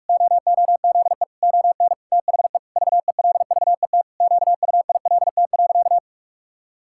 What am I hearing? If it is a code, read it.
Morse code, heard as a keyed tone, about 48 wpm, 700 Hz. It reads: OOZE ON THE VELVET QUILT.
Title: Pangram Challenge v2 - 45wpm